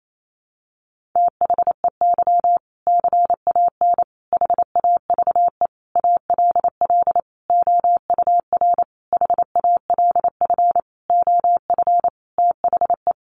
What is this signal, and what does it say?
Morse code, heard as a keyed tone, about 28 wpm, 700 Hz. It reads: THEY CAN HAVE ALL OUR HALF OF THE
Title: Sentences from Top 400 Words 28wpm